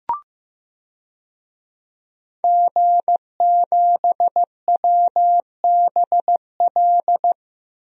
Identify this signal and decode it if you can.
Morse code, heard as a keyed tone, about 15 wpm, 700 Hz. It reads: G7WBL
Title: DXCC-100 Call Signs - 5 Characters - 15wpm